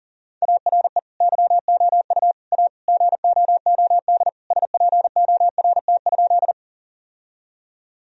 Morse code, heard as a keyed tone, about 40 wpm, 700 Hz. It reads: ARE YOU A GOOD SPORT?
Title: Sentences from Top 1000 Words - Review Part 4 - 40wpm